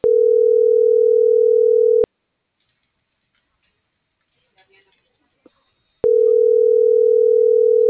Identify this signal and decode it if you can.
Ringback tone (North America), 440 + 480 Hz.